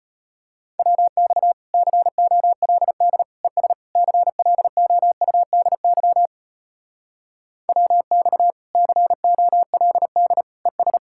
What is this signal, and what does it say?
Morse code, heard as a keyed tone, about 33 wpm, 700 Hz. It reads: WX COLD ES CLOUDY WX COLD ES